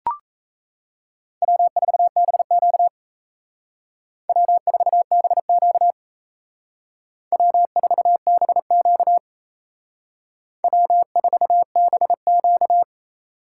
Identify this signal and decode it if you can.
Morse code, heard as a keyed tone, about 34 wpm, 700 Hz. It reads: W4BQ W4BQ W4BQ W4BQ
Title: Speed Racing - USA 1x2 Call Signs 28wpm